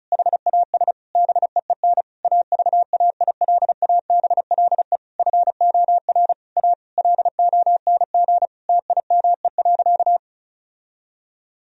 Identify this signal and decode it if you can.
Morse code, heard as a keyed tone, about 36 wpm, 700 Hz. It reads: HAS BEEN AVAILABLE FOR A LONG TIME.
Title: Sentences from Top 1000 Words - Review Part 4 - 35wpm